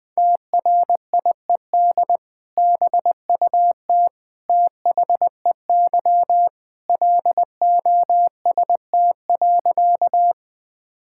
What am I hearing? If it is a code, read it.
Morse code, heard as a keyed tone, about 21 wpm, 700 Hz. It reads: TRIED BUT THEY LOST.
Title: Sentences with 2-Word Phrases - Spoken Beginning - 20wpm